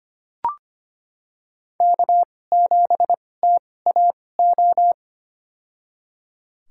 Morse code, heard as a keyed tone, about 25 wpm, spaced as for 18 wpm, 700 Hz. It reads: K7TAO
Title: USA Call Signs - 5 Characters - 2x Character Spacing - 25wpm